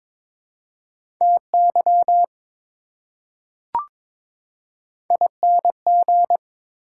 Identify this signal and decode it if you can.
Morse code, heard as a keyed tone, about 21 wpm, 700 Hz. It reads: TY ING
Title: Prefixes and Suffixes 22wpm